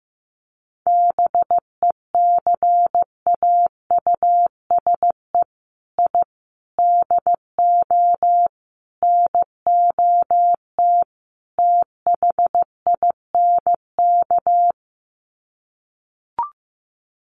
Morse code, heard as a keyed tone, about 15 wpm, 700 Hz. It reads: BECAUSE I DO NOT THINK
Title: Sets of 5 Words Encore 15wpm